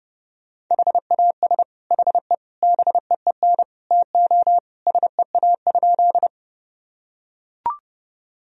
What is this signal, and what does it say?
Morse code, heard as a keyed tone, about 31 wpm, 700 Hz. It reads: HAS HE BEEN TO SEA?
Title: Sentences from Top 1000 Words - Review Part 1 - 30wpm